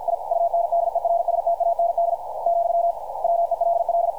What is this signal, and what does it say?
Morse code, heard as a keyed tone, about 50 wpm, 680 Hz, with radio noise. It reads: I UNDERSTAND , WELL
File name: archived / CRSnet